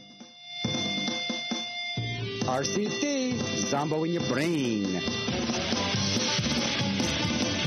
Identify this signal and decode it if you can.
Dial-up modem handshake, with speech.